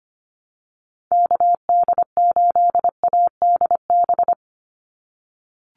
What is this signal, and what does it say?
Morse code, heard as a keyed tone, about 25 wpm, 700 Hz. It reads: KD8ADB
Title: USA Call Signs 25wpm